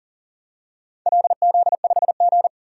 Morse code, heard as a keyed tone, about 44 wpm, 690 Hz. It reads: LZ5G
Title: DXCC-100 Call Signs - 4 Characters - 40wpm